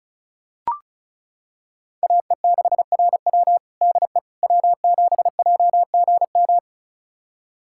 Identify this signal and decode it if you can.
Morse code, heard as a keyed tone, about 36 wpm, 700 Hz. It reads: AE6RW DE W7JGM
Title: USA Callsign DE Callsign - 5 Characters - 35wpm